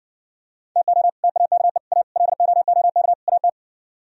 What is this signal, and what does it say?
Morse code, heard as a keyed tone, about 69 wpm, 700 Hz. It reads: TO TAKE A LOOK AT